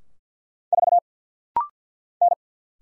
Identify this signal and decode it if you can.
Morse code, heard as a keyed tone, about 48 wpm, 700 Hz. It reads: 4 N
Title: Single Letter-Number Alt - Rapid-Fire 50wpm